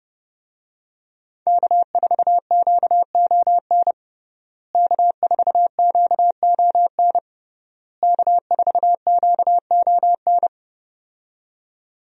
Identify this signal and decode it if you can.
Morse code, heard as a keyed tone, about 32 wpm, 700 Hz. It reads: K4QON K4QON K4QON
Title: USA Call Signs - 5 Characters - Repeated 3x - 30wpm